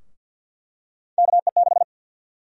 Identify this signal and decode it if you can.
Morse code, heard as a keyed tone, about 52 wpm, 700 Hz. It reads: KE6
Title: USA Call Sign Prefixes - 50wpm